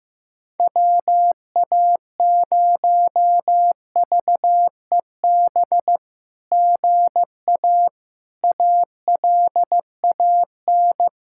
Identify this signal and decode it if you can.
Morse code, heard as a keyed tone, about 15 wpm, 700 Hz. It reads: WA0VEB GA ALAN
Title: SOTA QSO Practice 15wpm